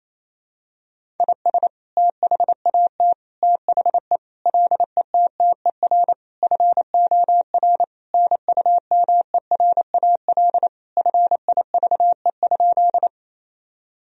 Morse code, heard as a keyed tone, about 28 wpm, 700 Hz. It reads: IS THAT THE LETTER FOR NUMERAL FIVE?
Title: Sentences from Top 1000 Words - Review Part 2 - 28wpm